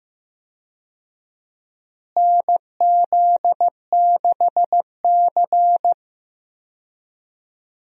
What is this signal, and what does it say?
Morse code, heard as a keyed tone, about 15 wpm, 700 Hz. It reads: NZ6C
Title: USA Call Signs - 4 Characters - Repeated 2x - 15wpm